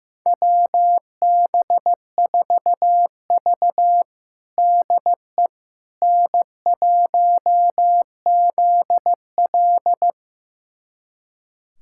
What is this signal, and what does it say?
Morse code, heard as a keyed tone, about 15 wpm, 700 Hz. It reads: WB4V DE N1ZL
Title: USA Callsign DE Callsign - 4 Characters - 15wpm